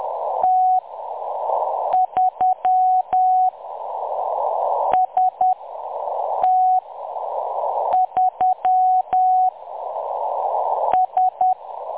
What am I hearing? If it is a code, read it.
Morse code, heard as a keyed tone, about 10 wpm, spaced as for 6 wpm, 740 Hz. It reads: T3 ST3 S
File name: M23-ST3- -0858utc-10184khz